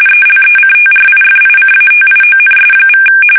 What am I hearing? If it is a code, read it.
RTTY, about 45.45 baud. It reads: ELECTRIFICATIONS=
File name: fsk850 / benjamin-g-lamme